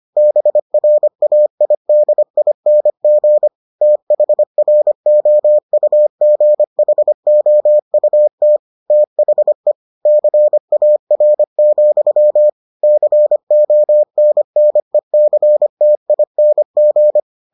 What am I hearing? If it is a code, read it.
Morse code, heard as a keyed tone, about 25 wpm, 600 Hz. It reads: BRAIDING THROUGHOUT THE CAR, CONNECTING